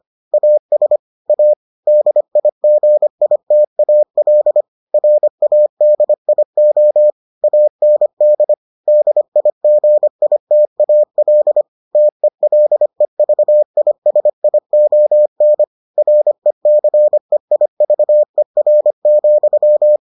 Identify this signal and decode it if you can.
Morse code, heard as a keyed tone, about 25 wpm, 600 Hz. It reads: AS A DIGITAL RADIO AND DIGITAL TELEVISION RECEIVER,